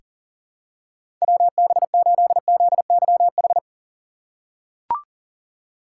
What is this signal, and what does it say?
Morse code, heard as a keyed tone, about 40 wpm, 700 Hz. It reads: WB8ZYH
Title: USA Call Signs - 6 Characters - 2x Character Spacing - 40wpm